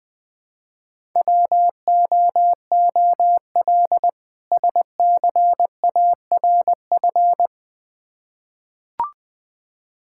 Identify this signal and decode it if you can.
Morse code, heard as a keyed tone, about 20 wpm, 700 Hz. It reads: WOOL SCARF
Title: Difficult Words with Hints - 20wpm